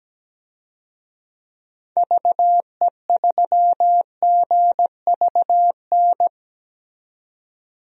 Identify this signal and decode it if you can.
Morse code, heard as a keyed tone, about 17 wpm, 700 Hz. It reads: VE3GVN
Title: DXCC-100 Call Signs - 6 Characters - 17wpm